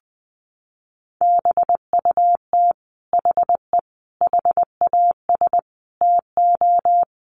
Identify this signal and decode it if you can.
Morse code, heard as a keyed tone, about 20 wpm, 700 Hz. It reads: BUT HE HAS TO
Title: Sentences from Top 600 Words 20wpm